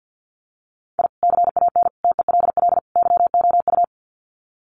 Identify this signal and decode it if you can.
Morse code, heard as a keyed tone, about 50 wpm, 700 Hz. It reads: I CAN TELL YOU
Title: Sets of 4 Words 50wpm